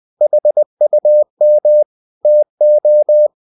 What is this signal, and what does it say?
Morse code, heard as a keyed tone, about 20 wpm, 600 Hz. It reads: HUM TO
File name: cw-books / winnie-the-pooh / winnie-20wpm-page-0013